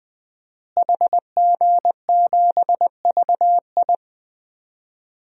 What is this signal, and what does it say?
Morse code, heard as a keyed tone, about 20 wpm, 700 Hz. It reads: HG7VI
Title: DXCC-100 Call Signs - 5 Characters - Repeated 3x - 20wpm